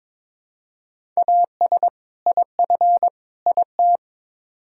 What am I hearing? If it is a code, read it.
Morse code, heard as a keyed tone, about 22 wpm, 700 Hz. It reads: AS IF IT
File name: practice / 22wpm / Build A Phrase - 5-Word Phrases - 22wpm